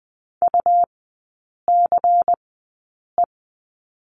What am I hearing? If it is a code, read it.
Morse code, heard as a keyed tone, about 20 wpm, spaced as for 8 wpm, 700 Hz. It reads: UCE
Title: 107 Lesson 18e - Letter m - Three Characters 20wpm